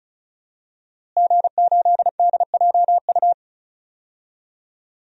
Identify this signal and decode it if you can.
Morse code, heard as a keyed tone, about 36 wpm, 700 Hz. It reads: G8DJU
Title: DXCC-100 Call Signs - 5 Characters - Repeated 3x - 35wpm